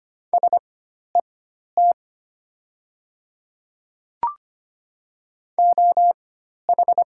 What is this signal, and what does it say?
Morse code, heard as a keyed tone, about 25 wpm, spaced as for 12 wpm, 700 Hz. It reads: SET OH
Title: 4x Character Spacing - Top 500 Words - 25wpm